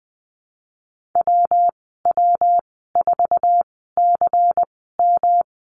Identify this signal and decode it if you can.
Morse code, heard as a keyed tone, about 20 wpm, spaced as for 15 wpm, 700 Hz. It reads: WW4CM
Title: 161 Lesson 26g - Letter g - Call Signs 20wpm